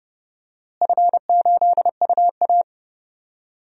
Morse code, heard as a keyed tone, about 30 wpm, 700 Hz. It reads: F8UA
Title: DXCC-100 Call Signs - 4 Characters - Repeated 3x - 30wpm